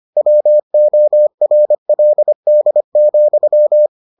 Morse code, heard as a keyed tone, about 25 wpm, 600 Hz. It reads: WORLD,